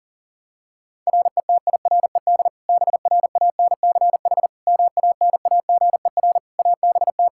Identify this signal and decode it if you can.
Morse code, heard as a keyed tone, about 40 wpm, 700 Hz. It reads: RETIRED BRANCH MANAGER ABT
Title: Practice QSO - Exchange 3 - More Info - 40wpm